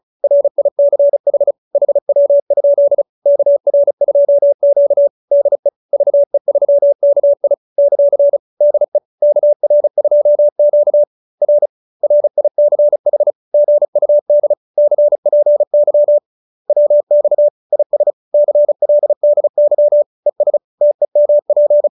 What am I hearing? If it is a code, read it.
Morse code, heard as a keyed tone, about 36 wpm, 590 Hz. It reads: RICH HW? KR2Q DE VE3KI ; DE KR2Q R RICH GUD CPY WX IS CLDY ES TEMP